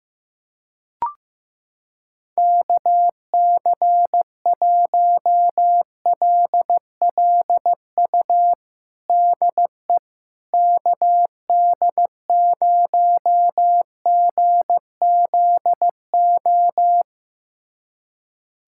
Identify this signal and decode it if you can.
Morse code, heard as a keyed tone, about 15 wpm, 700 Hz. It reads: KC1LLU DE KD0GZO
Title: USA Callsign DE Callsign 15wpm